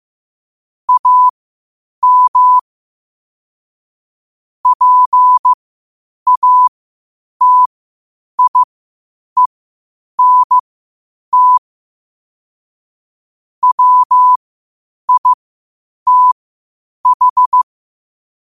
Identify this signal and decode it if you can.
Morse code, heard as a keyed tone, about 15 wpm, spaced as for 8 wpm, 1000 Hz. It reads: AM PATIENT WITH
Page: 8 wpm morse code quotes